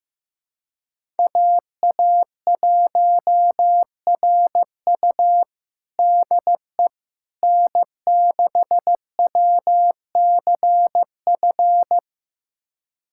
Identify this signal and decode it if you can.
Morse code, heard as a keyed tone, about 16 wpm, 700 Hz. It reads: AA1RU DE N6WCF
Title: USA Callsign DE Callsign - 5 Characters - 15wpm